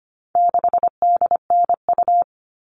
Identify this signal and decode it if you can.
Morse code, heard as a keyed tone, about 25 wpm, 700 Hz. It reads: 6DNU
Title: All Cut Numbers - 4-Digit 25wpm